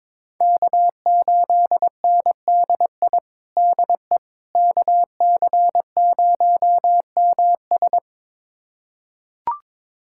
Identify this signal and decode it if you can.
Morse code, heard as a keyed tone, about 22 wpm, 700 Hz. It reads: K8NDI DE KC0MS
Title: USA Callsign DE Callsign - 5 Characters - 22wpm